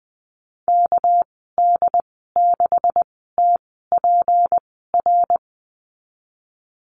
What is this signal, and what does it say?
Morse code, heard as a keyed tone, about 20 wpm, spaced as for 15 wpm, 700 Hz. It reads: KD6TPR
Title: 213 Lesson 34g - Number 8 - Call Signs 20wpm